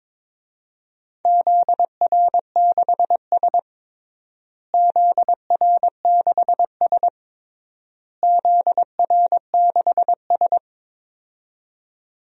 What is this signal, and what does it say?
Morse code, heard as a keyed tone, about 22 wpm, 700 Hz. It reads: ZR6S ZR6S ZR6S
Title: DXCC-100 Call Signs - 4 Characters - Repeated 3x - 22wpm